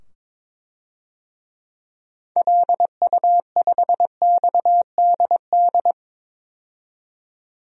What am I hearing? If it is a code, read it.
Morse code, heard as a keyed tone, about 22 wpm, 700 Hz. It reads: LU5XDD
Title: DXCC-100 Call Signs - 6 Characters - 22wpm